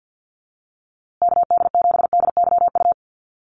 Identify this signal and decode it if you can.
Morse code, heard as a keyed tone, about 50 wpm, 700 Hz. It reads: KD7DYU
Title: USA Call Signs Repeated 3x 50wpm